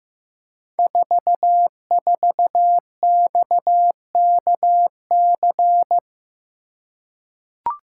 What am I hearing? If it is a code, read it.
Morse code, heard as a keyed tone, about 15 wpm, 700 Hz. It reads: 44XKC
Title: Code Groups - 5 Letter:Numbers - 15wpm